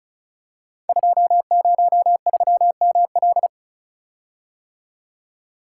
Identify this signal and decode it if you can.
Morse code, heard as a keyed tone, about 36 wpm, 700 Hz. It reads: 203ML
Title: Code Groups - 5 Letter:Numbers - 35wpm